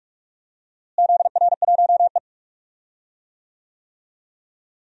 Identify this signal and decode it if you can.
Morse code, heard as a keyed tone, about 46 wpm, 700 Hz. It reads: ZR1E
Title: DXCC-100 Call Signs - 4 Characters - Repeated 2x - 45wpm